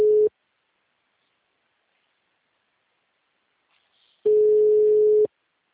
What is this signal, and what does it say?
Ringback tone (Europe, 425 Hz).
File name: out- -2011- - - .28